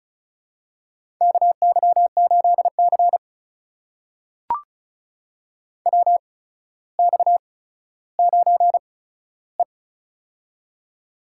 Morse code, heard as a keyed tone, about 36 wpm, spaced as for 15 wpm, 700 Hz. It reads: KY8C WX9E
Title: USA Call Signs - 4 Characters - 8x Character Spacing - 35wpm